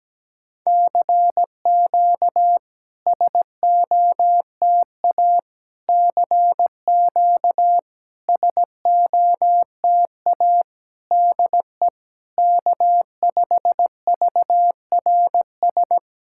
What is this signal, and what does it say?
Morse code, heard as a keyed tone, about 17 wpm, 690 Hz. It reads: CQ SOTA CQ SOTA DE K5VRS
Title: SOTA QSO Practice 17wpm